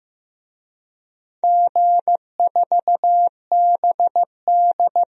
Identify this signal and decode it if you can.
Morse code, heard as a keyed tone, about 15 wpm, 700 Hz. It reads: G4BD
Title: DXCC-100 Call Signs - 4 Characters - 15wpm